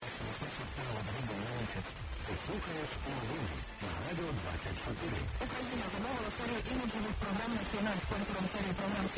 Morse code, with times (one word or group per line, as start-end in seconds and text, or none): none